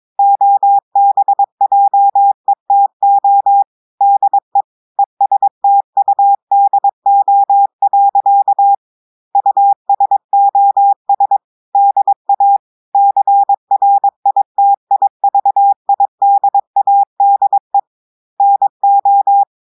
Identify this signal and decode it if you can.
Morse code, heard as a keyed tone, about 22 wpm, 800 Hz. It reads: OBJETO DE ESTUDO. USOS DA CRITIVIDADE NO
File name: audios / portugais / portugais texte en clair 1 vitesse 22 mots minutes